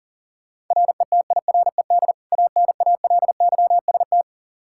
Morse code, heard as a keyed tone, about 40 wpm, 700 Hz. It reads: RETIRED ANALYST
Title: Practice QSO - Exchange 3 - More Info - 40wpm